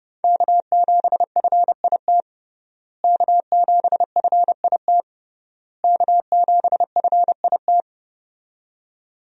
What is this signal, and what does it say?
Morse code, heard as a keyed tone, about 30 wpm, 700 Hz. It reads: K7FIT K7FIT K7FIT
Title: USA Call Signs - 5 Characters - Repeated 3x - 30wpm